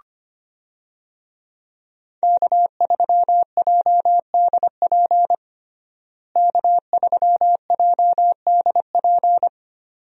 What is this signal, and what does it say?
Morse code, heard as a keyed tone, about 25 wpm, 700 Hz. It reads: K3JDP K3JDP
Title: USA Call Signs - 5 Characters - Repeated 2x - 25wpm